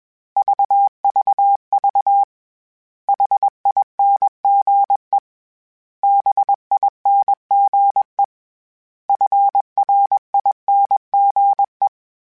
Morse code, heard as a keyed tone, about 21 wpm, 800 Hz. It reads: VVV HINGE BINGE FRINGE
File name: Special INGs / Special INGs-mp3s / Hidden ings @20